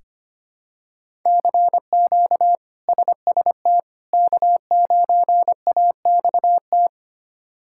Morse code, heard as a keyed tone, about 25 wpm, 700 Hz. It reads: CQ SST K9AXT
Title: K1USN SST QSO Practice - 25wpm